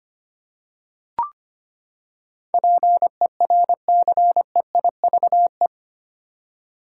Morse code, heard as a keyed tone, about 25 wpm, 700 Hz. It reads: PERCEIVE
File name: Eight Letter Words - 25wpm